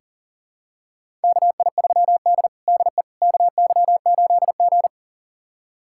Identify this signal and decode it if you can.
Morse code, heard as a keyed tone, about 40 wpm, 700 Hz. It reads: KI3D DE KY8G
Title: USA Callsign DE Callsign - 4 Characters - 40wpm